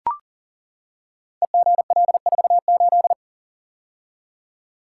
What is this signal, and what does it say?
Morse code, heard as a keyed tone, about 40 wpm, 690 Hz. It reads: EGL48
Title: Code Groups - 5 Letter:Numbers - 40wpm